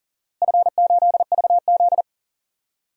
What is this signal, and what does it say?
Morse code, heard as a keyed tone, about 39 wpm, 700 Hz. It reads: F8VZ
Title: DXCC-100 Call Signs - 4 Characters - Repeated 3x - 40wpm